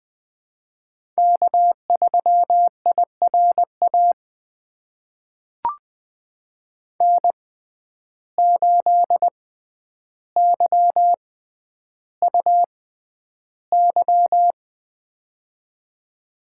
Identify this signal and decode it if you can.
Morse code, heard as a keyed tone, about 20 wpm, spaced as for 10 wpm, 700 Hz. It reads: K3IRA N8YUY
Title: USA Call Signs - 5 Characters - 6x Character Spacing - 20wpm